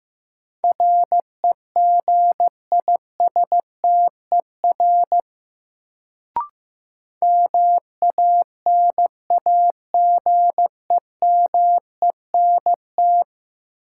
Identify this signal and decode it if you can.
Morse code, heard as a keyed tone, about 15 wpm, 700 Hz. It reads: REGISTER MANAGEMENT
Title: Sentences from Top 1200 Words - New Word Drill Alt - 15wpm